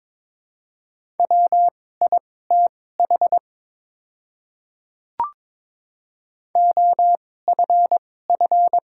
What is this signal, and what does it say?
Morse code, heard as a keyed tone, about 22 wpm, spaced as for 16 wpm, 700 Hz. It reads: WITH OFF
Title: 2x Character Spacing - Top 300 Words - 22wpm